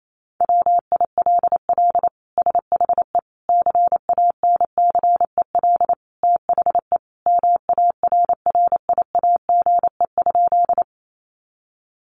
Morse code, heard as a keyed tone, about 28 wpm, 700 Hz. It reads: WILL SHE CANCEL THE MARRIAGE?
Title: Sentences from Top 1200 Words 28wpm